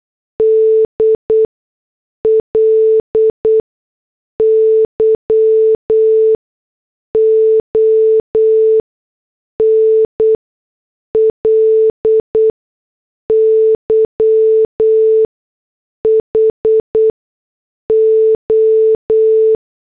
Morse code, heard as a keyed tone, about 8 wpm, spaced as for 6 wpm, 440 Hz. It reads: DLY ONLY HO